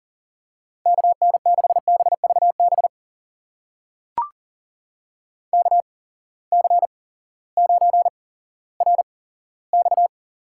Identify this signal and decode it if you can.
Morse code, heard as a keyed tone, about 40 wpm, spaced as for 18 wpm, 700 Hz. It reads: KN6BVB KC9RX
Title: USA Call Signs - 6 Characters - 8x Character Spacing - 40wpm